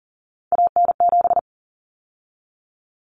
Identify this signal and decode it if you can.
Morse code, heard as a keyed tone, about 40 wpm, 700 Hz. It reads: AN7
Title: Common Cut Numbers - 3-Digit 40wpm